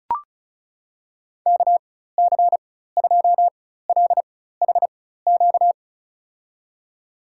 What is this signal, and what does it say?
Morse code, heard as a keyed tone, about 36 wpm, spaced as for 16 wpm, 700 Hz. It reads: KC2LHQ
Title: USA Call Signs - 6 Characters - 4x Character Spacing - 35wpm